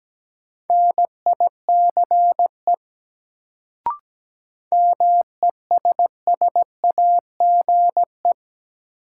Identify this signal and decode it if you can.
Morse code, heard as a keyed tone, about 17 wpm, 700 Hz. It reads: NICE MESSAGE
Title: Sentences from Top 1100 Words - New Word Drill Alt - 17wpm